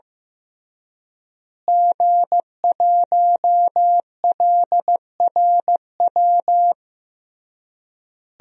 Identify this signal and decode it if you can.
Morse code, heard as a keyed tone, about 15 wpm, 700 Hz. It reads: G1LRW
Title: DXCC-100 Call Signs - 5 Characters - Repeated 3x - 15wpm